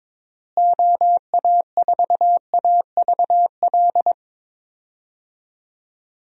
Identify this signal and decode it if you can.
Morse code, heard as a keyed tone, about 22 wpm, 700 Hz. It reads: OA4AVL
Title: USA Call Signs - 6 Characters - Repeated 3x - 22wpm